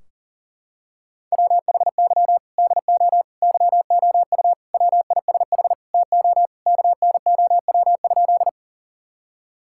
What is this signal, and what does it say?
Morse code, heard as a keyed tone, about 40 wpm, 700 Hz. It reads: WHY DO YOU WISH TO KNOW?
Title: Sentences from Top 1600 Words 40wpm